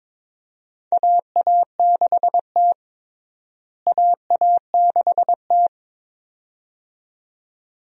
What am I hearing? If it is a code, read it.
Morse code, heard as a keyed tone, about 22 wpm, 700 Hz. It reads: AA6T AA6T
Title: USA Call Signs - 4 Characters - Repeated 2x - 22wpm